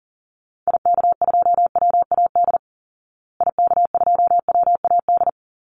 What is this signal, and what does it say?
Morse code, heard as a keyed tone, about 40 wpm, 700 Hz. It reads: IK2WAD IK2WAD
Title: Call Signs Repeated 3x - 40wpm